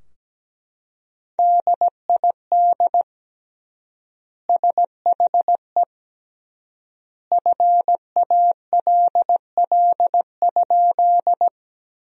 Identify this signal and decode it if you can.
Morse code, heard as a keyed tone, about 17 wpm, 700 Hz. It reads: DID SHE FALL?
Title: 3x Word-Spacing - Sentences from Top 500 Words - 17wpm